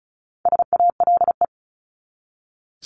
Morse code, heard as a keyed tone, about 34 wpm, 700 Hz. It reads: SALE
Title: Top 1251-1500 Words 35wpm